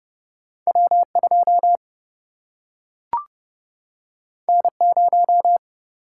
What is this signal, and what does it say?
Morse code, heard as a keyed tone, about 30 wpm, 700 Hz. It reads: W 2 N 0
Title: Single-Letter Single-Number 30wpm